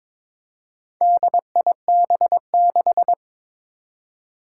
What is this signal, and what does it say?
Morse code, heard as a keyed tone, about 22 wpm, 700 Hz. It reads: DIB6
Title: Four Challenging Letter-Number Combinations 22wpm